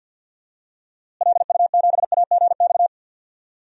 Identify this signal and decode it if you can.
Morse code, heard as a keyed tone, about 53 wpm, 700 Hz. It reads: LU7AGX
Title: DXCC-100 Call Signs - 6 Characters - Repeated 3x - 50wpm